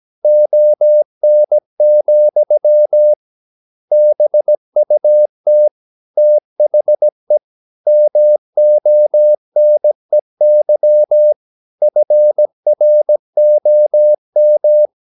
Morse code, heard as a keyed tone, about 17 wpm, 600 Hz. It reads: ON, BUT THE MONEY FROM
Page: Romanen Never af Ken Follet på engelsk.